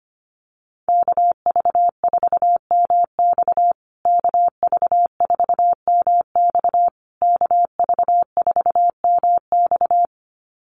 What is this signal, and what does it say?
Morse code, heard as a keyed tone, about 25 wpm, 700 Hz. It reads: KV4MX KV4MX KV4MX
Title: USA Call Signs Repeated 3x 25wpm